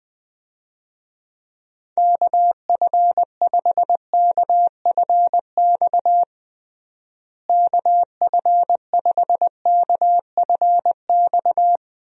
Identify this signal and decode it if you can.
Morse code, heard as a keyed tone, about 20 wpm, 700 Hz. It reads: KF5KFX KF5KFX
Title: USA Call Signs - 6 Characters - Repeated 3x - 20wpm